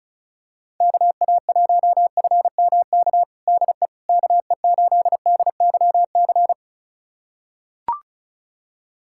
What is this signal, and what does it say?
Morse code, heard as a keyed tone, about 36 wpm, 690 Hz. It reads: KA1FMK DE KE8DYC
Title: USA Callsign DE Callsign - 6 Characters - 35wpm